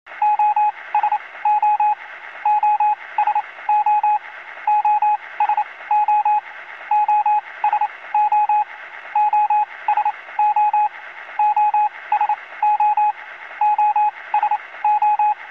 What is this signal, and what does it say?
Morse code, heard as a keyed tone, about 28 wpm, spaced as for 20 wpm, 840 Hz, with radio noise. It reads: OSO OSO OSO OSO OSO OSO OSO
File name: radio signals / libsounds.com - Morse code SOS